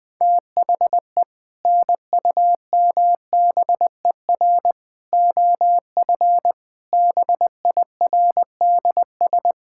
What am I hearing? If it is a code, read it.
Morse code, heard as a keyed tone, about 20 wpm, 700 Hz. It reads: THE NUMBER OF BIRDS
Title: Sentences with 3-Word Phrases - Spoken Beginning - 20wpm